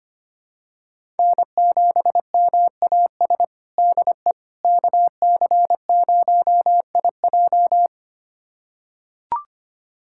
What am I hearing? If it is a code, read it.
Morse code, heard as a keyed tone, about 25 wpm, 700 Hz. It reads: N7MAS DE KC0IJ
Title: USA Callsign DE Callsign - 5 Characters - 25wpm